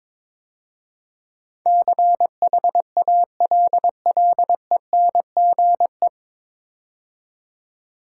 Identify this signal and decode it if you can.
Morse code, heard as a keyed tone, about 22 wpm, 700 Hz. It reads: CHALLENGE
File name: 9 Letter Words - First 3 Letters Given - 22wpm